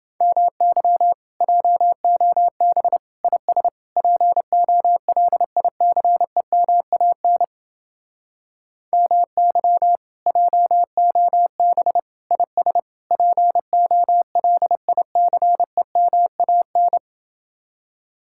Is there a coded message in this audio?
Yes: Morse code from 0.2 to 17.0 s, heard as a keyed tone, about 30 wpm, 700 Hz. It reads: MY JOB IS POLICEMAN MY JOB IS POLICEMAN